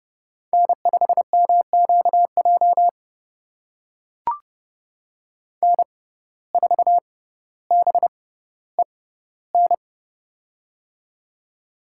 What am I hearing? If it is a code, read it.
Morse code, heard as a keyed tone, about 32 wpm, spaced as for 16 wpm, 700 Hz. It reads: N5MQJ N4BEN
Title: USA Call Signs - 5 Characters - 6x Character Spacing - 30wpm